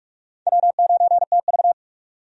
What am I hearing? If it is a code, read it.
Morse code, heard as a keyed tone, about 46 wpm, 700 Hz. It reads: W9TV
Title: USA Call Signs - 4 Characters - 45wpm